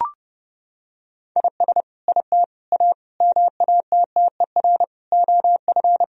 Morse code, heard as a keyed tone, about 30 wpm, 700 Hz. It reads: IS IT A MATTER OF
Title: Sentences from Top 1000 Words - Review Part 3 - 30wpm